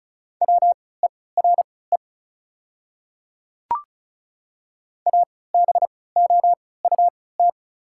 Morse code, heard as a keyed tone, about 36 wpm, spaced as for 20 wpm, 690 Hz. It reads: WERE ABOUT
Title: 3x Character Spacing - Top 200 Words - 35wpm